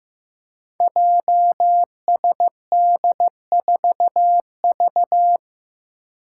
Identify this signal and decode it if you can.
Morse code, heard as a keyed tone, about 15 wpm, 700 Hz. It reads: JSD4V
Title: Five Challenging Letter-Number Combinations 15wpm